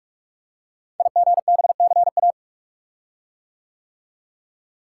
Morse code, heard as a keyed tone, about 44 wpm, 700 Hz. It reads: IGBCA
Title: Code Groups - 5 Letters - 45wpm